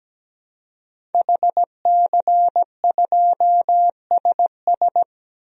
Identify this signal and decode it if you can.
Morse code, heard as a keyed tone, about 17 wpm, 700 Hz. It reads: HC2SS
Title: DXCC-100 Call Signs - 5 Characters - 17wpm